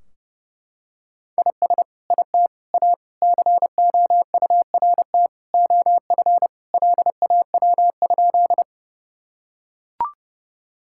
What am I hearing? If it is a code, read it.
Morse code, heard as a keyed tone, about 30 wpm, 700 Hz. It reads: IS IT A COURT OF LAW?
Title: Sentences from Top 1600 Words 30wpm